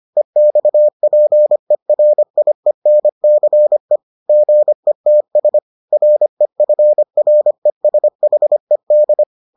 Morse code, heard as a keyed tone, about 25 wpm, 600 Hz. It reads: EXPERIENCE GETS REFRESHED